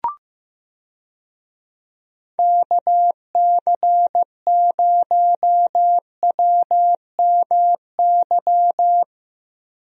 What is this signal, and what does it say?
Morse code, heard as a keyed tone, about 15 wpm, 700 Hz. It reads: KC0WMY
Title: USA Call Signs - 6 Characters - Repeated 2x - 15wpm